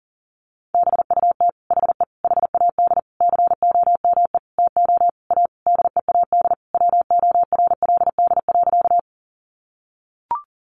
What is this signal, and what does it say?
Morse code, heard as a keyed tone, about 40 wpm, 700 Hz. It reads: BUT HE HAD COME TO A DEAD WORLD.
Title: Sentences from Top 1500 Words 40wpm